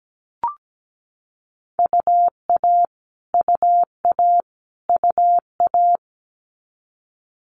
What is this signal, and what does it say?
Morse code, heard as a keyed tone, about 17 wpm, 700 Hz. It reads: UA UA UA
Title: Multi-letter Phonemes Repeated 3x 17wpm